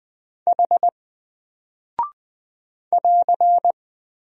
Morse code, heard as a keyed tone, about 20 wpm, 700 Hz. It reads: H <AR>
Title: Single Character (Letter:Number:Punctuation:Prosign) 20wpm